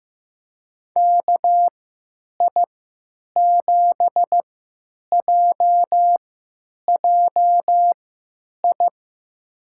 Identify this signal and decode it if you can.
Morse code, heard as a keyed tone, about 15 wpm, spaced as for 9 wpm, 700 Hz. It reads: KI7JJI
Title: USA Call Signs - 6 Characters - 3x Character Spacing - 15wpm